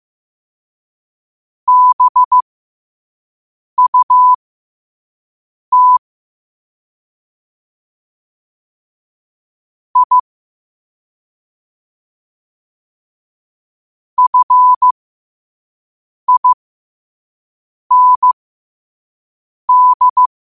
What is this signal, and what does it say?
Morse code, heard as a keyed tone, about 15 wpm, spaced as for 5 wpm, 1000 Hz. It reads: BUT I FIND
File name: qotdfiles / 5wpm